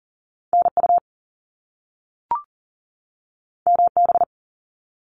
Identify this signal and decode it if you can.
Morse code, heard as a keyed tone, about 40 wpm, 700 Hz. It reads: NU MB
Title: Canadian Provinces and Territories 40wpm